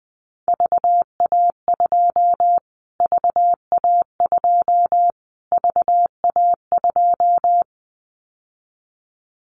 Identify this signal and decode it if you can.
Morse code, heard as a keyed tone, about 19 wpm, 700 Hz. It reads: VA2 VA2 VA2
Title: Callsign Prefixes for Top 25 DXCC - Repeated 3x - 20wpm